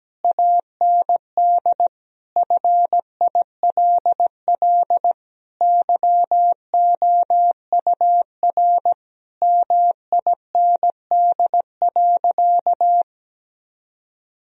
Morse code, heard as a keyed tone, about 17 wpm, 700 Hz. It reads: AND FILL YOUR MIND.
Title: Sentences from Top 1000 Words - Review Part 3 - 17wpm